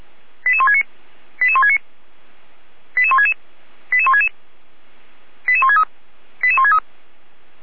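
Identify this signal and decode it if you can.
Five-tone selective call (ZVEI), heard as quick runs of five tones, tones of about 70 ms. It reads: 88169 88169 88166 88166 88163 88163
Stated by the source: ZVEI five-tone call 88352 received with an SDR.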